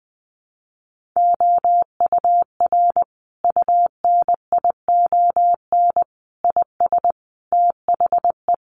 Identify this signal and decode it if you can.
Morse code, heard as a keyed tone, about 20 wpm, 700 Hz. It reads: OUR UNION IS THE
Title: Sentences from Top 1400 Words 20wpm